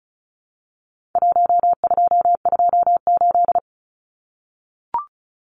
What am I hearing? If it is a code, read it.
Morse code, heard as a keyed tone, about 34 wpm, 700 Hz. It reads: 1228
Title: Numbers - Contest Style - 4-Digit 35wpm